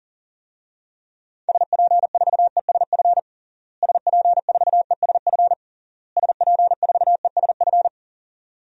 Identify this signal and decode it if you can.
Morse code, heard as a keyed tone, about 40 wpm, 700 Hz. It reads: SP4ESF SP4ESF SP4ESF
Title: DXCC-100 Call Signs - 6 Characters - Repeated 3x - 40wpm